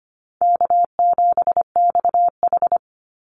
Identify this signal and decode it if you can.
Morse code, heard as a keyed tone, about 25 wpm, 700 Hz. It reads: K7XH
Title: Easy Call Signs - 25wpm